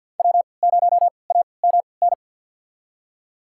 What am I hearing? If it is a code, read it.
Morse code, heard as a keyed tone, about 53 wpm, spaced as for 29 wpm, 690 Hz. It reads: W0AMN
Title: USA Call Signs - 5 Characters - 3x Character Spacing - 50wpm